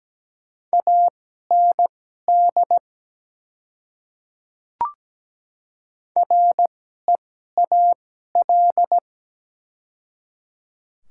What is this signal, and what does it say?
Morse code, heard as a keyed tone, about 17 wpm, spaced as for 12 wpm, 700 Hz. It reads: AND REAL
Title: 2x Character Spacing - Top 300 Words - 17wpm